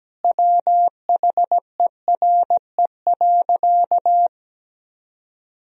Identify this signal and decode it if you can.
Morse code, heard as a keyed tone, about 17 wpm, 700 Hz. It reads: WHERE.
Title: Sentences from Top 1000 Words - Review Part 1 - 17wpm